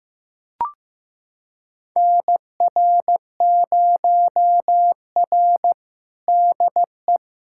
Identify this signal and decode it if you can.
Morse code, heard as a keyed tone, about 15 wpm, 700 Hz. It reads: NR0R DE
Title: USA Callsign DE Callsign - 4 Characters - 15wpm